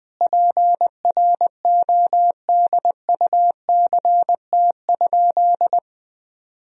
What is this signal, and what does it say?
Morse code, heard as a keyed tone, about 20 wpm, 700 Hz. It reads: PRODUCT?
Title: Sentences from Top 1000 Words - Review Part 2 - 20wpm